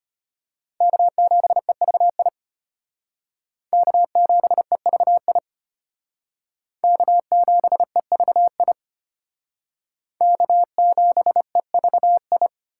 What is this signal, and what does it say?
Morse code, heard as a keyed tone, about 32 wpm, 700 Hz. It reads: K7EVI K7EVI K7EVI K7EVI